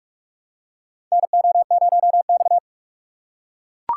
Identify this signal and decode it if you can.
Morse code, heard as a keyed tone, about 44 wpm, 700 Hz. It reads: NO0X
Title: USA Call Signs - 4 Characters - 6x Character Spacing - 45wpm